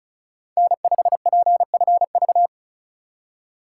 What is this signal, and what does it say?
Morse code, heard as a keyed tone, about 34 wpm, 700 Hz. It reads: N5PFV
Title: USA Call Signs - 4x Character Spacing - 35wpm